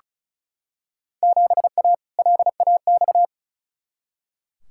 Morse code, heard as a keyed tone, about 36 wpm, 700 Hz. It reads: 7A LAX
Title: Field Day Exchanges Encore - 35wpm